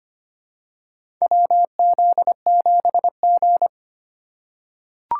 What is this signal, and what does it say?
Morse code, heard as a keyed tone, about 25 wpm, 700 Hz. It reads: WZ7G
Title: USA Call Signs - 4 Characters - 2x Character Spacing - 25wpm